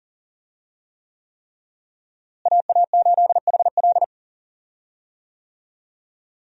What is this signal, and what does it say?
Morse code, heard as a keyed tone, about 40 wpm, 700 Hz. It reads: AA8HL
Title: USA Call Signs - 5 Characters - Repeated 3x - 40wpm